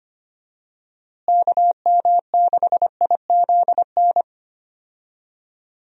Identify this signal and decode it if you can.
Morse code, heard as a keyed tone, about 25 wpm, 700 Hz. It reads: KM6IZN
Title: USA Call Signs - 6 Characters - Repeated 3x - 25wpm